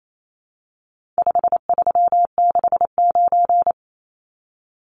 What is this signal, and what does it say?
Morse code, heard as a keyed tone, about 28 wpm, 700 Hz. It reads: 5369
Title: Numbers - 4-Digit 28wpm